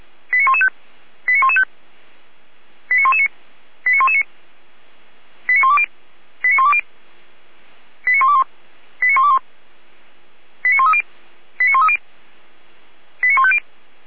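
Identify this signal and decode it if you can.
Five-tone selective call (ZVEI), heard as quick runs of five tones, tones of about 70 ms. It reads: 89115 89115 89119 89119 89120 89120 89121 89121 89130 89130 89150
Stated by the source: ZVEI five-tone selective call received on SDR.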